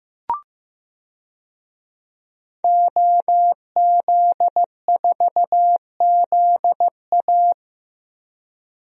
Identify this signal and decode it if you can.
Morse code, heard as a keyed tone, about 16 wpm, 700 Hz. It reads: OZ4ZA
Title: DXCC-100 Call Signs - 5 Characters - Repeated 2x - 15wpm